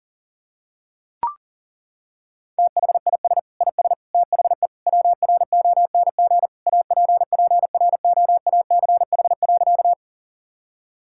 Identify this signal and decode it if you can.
Morse code, heard as a keyed tone, about 40 wpm, 700 Hz. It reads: THIS IS THE WRONG APPROACH.
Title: Sentences from Top 1000 Words - Review Part 6 - 40wpm